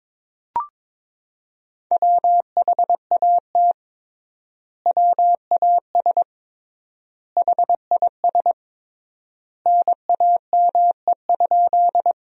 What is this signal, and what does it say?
Morse code, heard as a keyed tone, about 22 wpm, 700 Hz. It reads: WHAT WAS HIS NAME?
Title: 3x Word-Spacing - Sentences from Top 200 Words - 22wpm